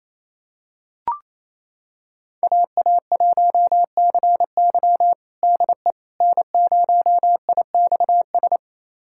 Morse code, heard as a keyed tone, about 28 wpm, 700 Hz. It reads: AA1CY DE N0IXS
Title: USA Callsign DE Callsign - 5 Characters - 28wpm